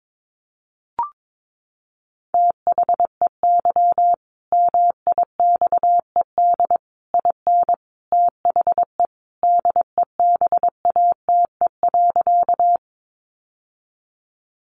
Morse code, heard as a keyed tone, about 22 wpm, 700 Hz. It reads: THEY MIXED IN THE DEBATE.
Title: Sentences from Top 1500 Words 22wpm